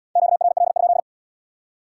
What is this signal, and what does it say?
Morse code, heard as a keyed tone, about 70 wpm, 700 Hz. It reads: QRL?